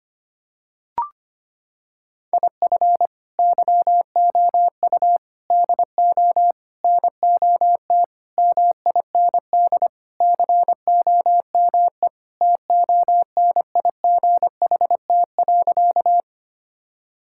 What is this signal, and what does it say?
Morse code, heard as a keyed tone, about 25 wpm, 700 Hz. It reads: IF YOU DO NOT MIND COME TONIGHT.
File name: practice / 25wpm / Sentences with 4-Word Phrases - 25wpm